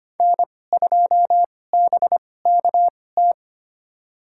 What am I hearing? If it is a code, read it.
Morse code, heard as a keyed tone, about 25 wpm, spaced as for 18 wpm, 700 Hz. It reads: N2BKT
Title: USA Call Signs - 5 Characters - 2x Character Spacing - 25wpm